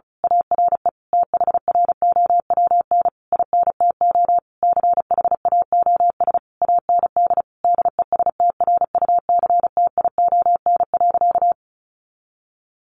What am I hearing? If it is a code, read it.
Morse code, heard as a keyed tone, about 34 wpm, 700 Hz. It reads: ARE THROWN INTO CHAOS AND DESTRUCTION.
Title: A Sound Of Thunder 35wpm